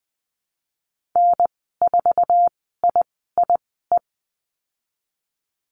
Morse code, heard as a keyed tone, about 20 wpm, spaced as for 14 wpm, 700 Hz. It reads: N4IIE
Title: 069 Lesson 12g - Letter d - Call Signs 20wpm